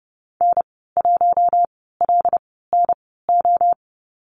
Morse code, heard as a keyed tone, about 30 wpm, spaced as for 17 wpm, 700 Hz. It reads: N1LNO
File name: Lesson 15g - Number 5 - Call Signs 30wpm